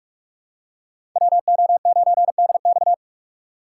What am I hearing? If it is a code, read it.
Morse code, heard as a keyed tone, about 46 wpm, 700 Hz. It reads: WO9DX
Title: USA Call Signs - 5 Characters - 3x Character Spacing - 45wpm